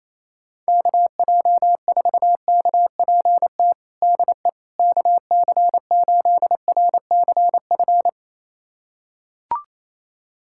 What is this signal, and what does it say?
Morse code, heard as a keyed tone, about 28 wpm, 700 Hz. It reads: KJ4KPT DE KC8RCF
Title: USA Callsign DE Callsign - 6 Characters - 28wpm